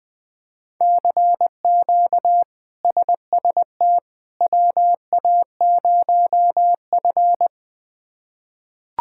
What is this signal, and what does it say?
Morse code, heard as a keyed tone, about 20 wpm, 700 Hz. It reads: CQ SST WA0F
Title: K1USN SST QSO Practice - 20wpm